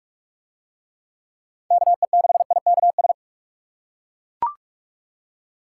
Morse code, heard as a keyed tone, about 41 wpm, 700 Hz. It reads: KE6IKS